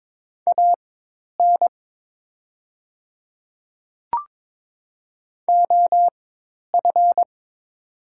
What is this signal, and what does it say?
Morse code, heard as a keyed tone, about 22 wpm, spaced as for 10 wpm, 700 Hz. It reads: AN OF
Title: 4x Character Spacing - Top 100 Words - 22wpm